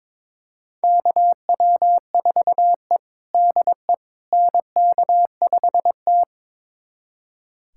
Morse code, heard as a keyed tone, about 22 wpm, 700 Hz. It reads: KW4E DE NK5T
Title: USA Callsign DE Callsign - 4 Characters - 22wpm